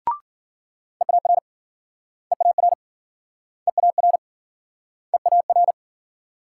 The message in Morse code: EAR EAR EAR EAR

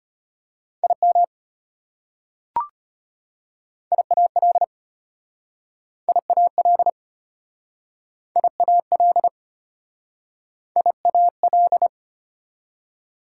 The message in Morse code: IM IAL IAL IAL IAL

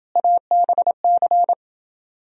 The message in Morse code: ABC

